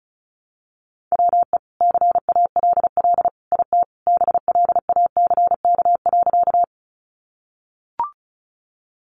Morse code, heard as a keyed tone, about 34 wpm, 700 Hz. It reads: WE CALL IT BLACK.
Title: Sentences from Top 400 Words 35wpm